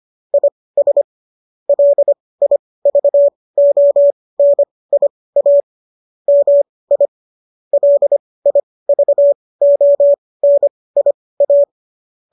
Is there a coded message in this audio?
Yes: Morse code from 0.3 to 11.6 s, heard as a keyed tone, about 25 wpm, spaced as for 18 wpm, 570 Hz. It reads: IS LIVONIA MI LIVONIA